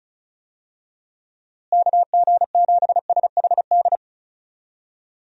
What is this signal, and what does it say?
Morse code, heard as a keyed tone, about 36 wpm, 700 Hz. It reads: KG7SHD
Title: USA Call Signs - 6 Characters - Repeated 2x - 35wpm